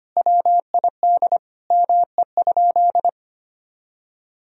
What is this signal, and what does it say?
Morse code, heard as a keyed tone, about 25 wpm, 700 Hz. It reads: WID ME?
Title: Sentences from Top 400 Words and Abbreviations 25wpm